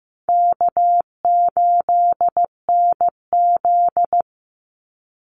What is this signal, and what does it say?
Morse code, heard as a keyed tone, about 15 wpm, 700 Hz. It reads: K8NZ
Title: Easy Call Signs Repeated 2x - 15wpm